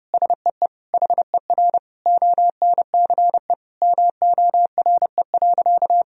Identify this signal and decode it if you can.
Morse code, heard as a keyed tone, about 32 wpm, 700 Hz. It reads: SEE HER ONCE MORE.
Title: Sentences from Top 500 Words and Abbreviations 30wpm